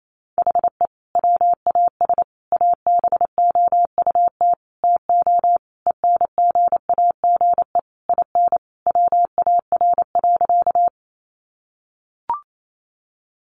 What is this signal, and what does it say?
Morse code, heard as a keyed tone, about 28 wpm, 700 Hz. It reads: HE WAS ABOUT TO ENGAGE IN WAR.
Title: Sentences from Top 1500 Words 28wpm